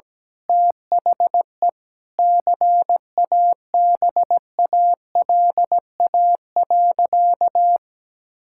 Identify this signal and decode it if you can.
Morse code, heard as a keyed tone, about 17 wpm, 700 Hz. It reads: THE CABALA.
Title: Pangram Challenge v2 - 17wpm